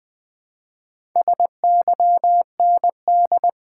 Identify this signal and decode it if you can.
Morse code, heard as a keyed tone, about 20 wpm, 700 Hz. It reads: SYND
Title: Build A Word - 2 Letters At A Time - 8-Letter Words - 20wpm